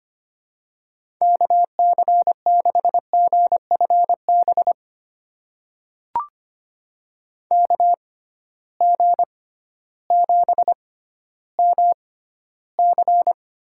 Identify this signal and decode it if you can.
Morse code, heard as a keyed tone, about 25 wpm, spaced as for 13 wpm, 700 Hz. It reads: KC6GFB KG7MC
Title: USA Call Signs - 6 Characters - 6x Character Spacing - 25wpm